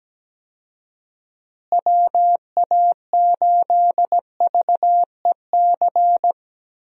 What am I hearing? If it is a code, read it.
Morse code, heard as a keyed tone, about 17 wpm, 700 Hz. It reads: WA8VEC
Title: USA Call Signs - 6 Characters - Repeated 2x - 17wpm